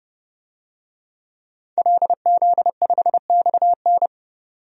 Morse code, heard as a keyed tone, about 30 wpm, 700 Hz. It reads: LZ5XN
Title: DXCC-100 Call Signs - 5 Characters - 30wpm